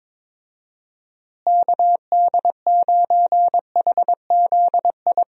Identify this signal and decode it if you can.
Morse code, heard as a keyed tone, about 22 wpm, 700 Hz. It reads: KD9HZI
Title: USA Call Signs - 6 Characters - Repeated 2x - 22wpm